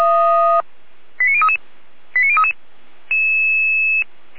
Five-tone selective call (ZVEI), heard as quick runs of five tones, tones of about 70 ms. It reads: 89033 89033